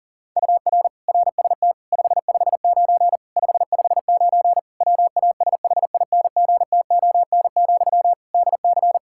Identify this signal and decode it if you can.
Morse code, heard as a keyed tone, about 40 wpm, 700 Hz. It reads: UR RST 559 559 WASHINGTON, DC
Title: Practice QSO - Exchange 1 - Beginning - 40wpm